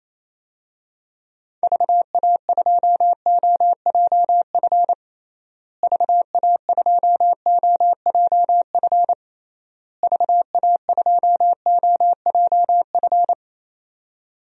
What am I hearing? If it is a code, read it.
Morse code, heard as a keyed tone, about 28 wpm, 700 Hz. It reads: VA2OJF VA2OJF VA2OJF
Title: DXCC-100 Call Signs - 6 Characters - Repeated 3x - 28wpm